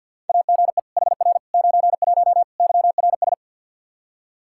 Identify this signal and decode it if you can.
Morse code, heard as a keyed tone, about 52 wpm, 700 Hz. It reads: AGE HR 91 YRS
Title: Practice QSO - Exchange 3 - More Info - 50wpm